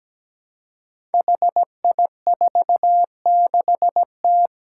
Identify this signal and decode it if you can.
Morse code, heard as a keyed tone, about 17 wpm, 700 Hz. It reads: HI46T
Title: Five Challenging Letter-Number Combinations 17wpm